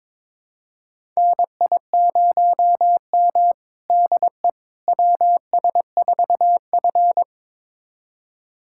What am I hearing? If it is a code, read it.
Morse code, heard as a keyed tone, about 22 wpm, 700 Hz. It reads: NI0M DE WS4F
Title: USA Callsign DE Callsign - 4 Characters - 22wpm